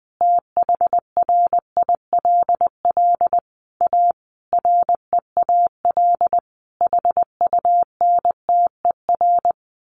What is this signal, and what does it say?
Morse code, heard as a keyed tone, about 20 wpm, 700 Hz. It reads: THRILL A REAL HUNTER
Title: A Sound Of Thunder 20wpm